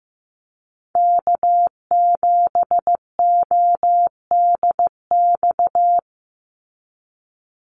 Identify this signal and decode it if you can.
Morse code, heard as a keyed tone, about 15 wpm, 700 Hz. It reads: K7ODX
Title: NA SOTA Activator Callsigns 15wpm